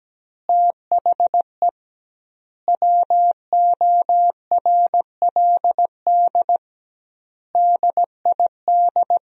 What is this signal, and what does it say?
Morse code, heard as a keyed tone, about 17 wpm, 700 Hz. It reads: THE WORLD DID
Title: 2x Word-Spacing - Sentences from Top 200 Words - 17wpm